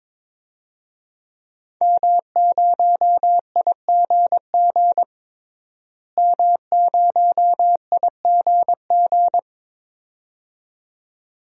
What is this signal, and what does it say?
Morse code, heard as a keyed tone, about 21 wpm, 700 Hz. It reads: M0IGG M0IGG
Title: DXCC-100 Call Signs - 5 Characters - Repeated 2x - 22wpm